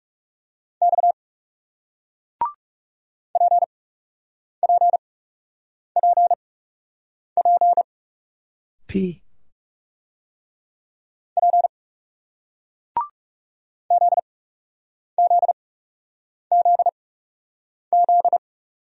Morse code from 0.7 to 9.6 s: X PPPP

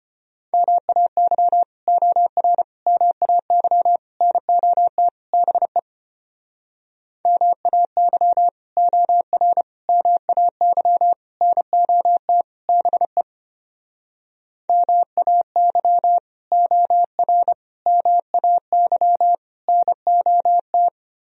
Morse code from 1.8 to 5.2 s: OR MAY NOT

